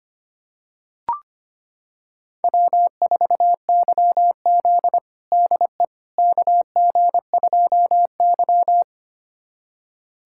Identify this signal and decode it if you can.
Morse code, heard as a keyed tone, about 25 wpm, 700 Hz. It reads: W4YZ DE KG2Y
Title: USA Callsign DE Callsign - 4 Characters - 25wpm